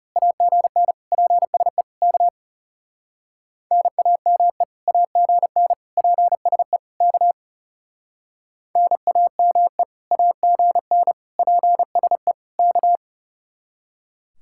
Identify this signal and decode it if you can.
Morse code, heard as a keyed tone, about 34 wpm, 700 Hz. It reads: AGN PSE K NAME AGN PSE K NAME AGN PSE K
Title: Speed-Racing Common QSO Phrases Encore - 30wpm